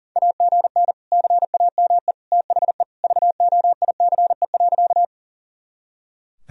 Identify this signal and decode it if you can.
Morse code, heard as a keyed tone, about 37 wpm, 700 Hz. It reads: AGN CAME THE VOICE.
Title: Sentences from Top 500 Words and Abbreviations 40wpm